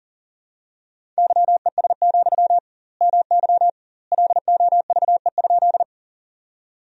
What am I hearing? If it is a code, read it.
Morse code, heard as a keyed tone, about 37 wpm, 700 Hz. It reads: YES, MY LOVE?
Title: 2x Word-Spacing - Sentences from Top 500 Words - 40wpm